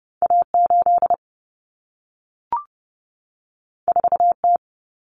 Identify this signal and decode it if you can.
Morse code, heard as a keyed tone, about 30 wpm, 700 Hz. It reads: A8 4T
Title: Common Cut Numbers - 2-Digit 30wpm